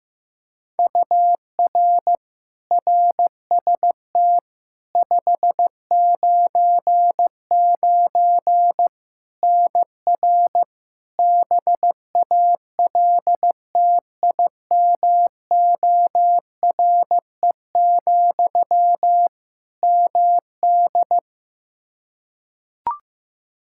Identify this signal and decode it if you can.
Morse code, heard as a keyed tone, about 15 wpm, 700 Hz. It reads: UR RST 599 NR BALTIMORE, MD
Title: Practice QSO - Exchange 1 - Beginning - 15wpm